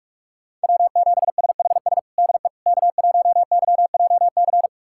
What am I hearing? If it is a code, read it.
Morse code, heard as a keyed tone, about 46 wpm, 700 Hz. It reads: W7SHS DE K1YJC
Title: USA Callsign DE Callsign - 5 Characters - 45wpm